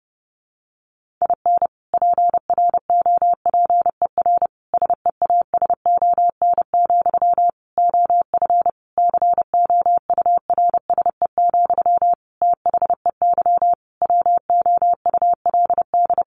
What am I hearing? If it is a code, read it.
Morse code, heard as a keyed tone, about 30 wpm, 700 Hz. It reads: IN PROPER SEASON, OF COURSE, THEY WOULD
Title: Sentences from Top 1400 Words 30wpm